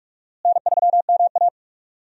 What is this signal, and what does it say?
Morse code, heard as a keyed tone, about 46 wpm, 700 Hz. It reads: N3MA
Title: USA Call Signs - 4 Characters - Repeated 3x - 45wpm